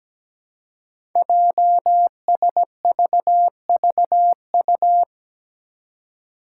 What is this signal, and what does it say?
Morse code, heard as a keyed tone, about 17 wpm, 700 Hz. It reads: JSVVU
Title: Five Challenging Letter-Number Combinations 17wpm